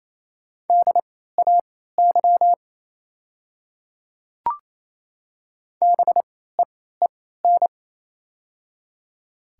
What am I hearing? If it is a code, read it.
Morse code, heard as a keyed tone, about 28 wpm, spaced as for 16 wpm, 700 Hz. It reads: DAY BEEN
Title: 3x Character Spacing - Top 200 Words - 28wpm